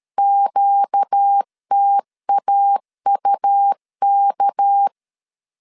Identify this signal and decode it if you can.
Morse code, heard as a keyed tone, about 13 wpm, 800 Hz. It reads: QTAUK
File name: LEZIONI / 6 PKX0 / 6 PK 12wpm 2006